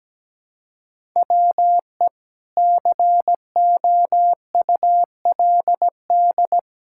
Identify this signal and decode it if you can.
Morse code, heard as a keyed tone, about 17 wpm, 700 Hz. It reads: WE COULD
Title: Pangram Challenge v2 - 17wpm